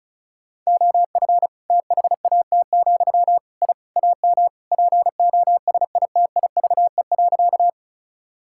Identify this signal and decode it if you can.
Morse code, heard as a keyed tone, about 36 wpm, 700 Hz. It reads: OF THAT, I AM POSITIVE.
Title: Sentences from Top 1000 Words - Review Part 5 - 35wpm